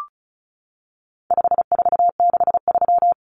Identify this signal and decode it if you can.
Morse code, heard as a keyed tone, about 34 wpm, 700 Hz. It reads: 5463
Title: Numbers - 4-Digit 35wpm